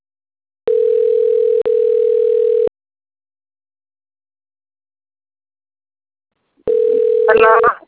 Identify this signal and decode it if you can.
Ringback tone (North America), 440 + 480 Hz.